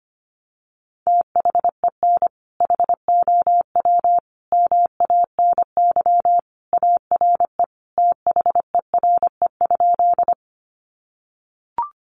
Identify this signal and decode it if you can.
Morse code, heard as a keyed tone, about 25 wpm, 700 Hz. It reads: THEN HOW MANY ARE THERE?
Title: Sentences from Top 100 Words 25wpm